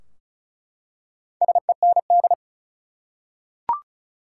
Morse code, heard as a keyed tone, about 34 wpm, 700 Hz. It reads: SEND
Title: Four Letter Words - 35wpm